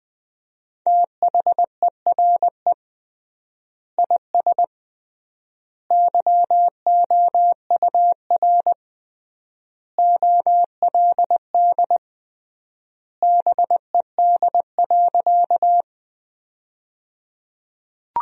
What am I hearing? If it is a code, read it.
Morse code, heard as a keyed tone, about 21 wpm, 700 Hz. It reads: THERE IS YOUR OLD BED.
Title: 3x Word-Spacing - Sentences from Top 500 Words - 20wpm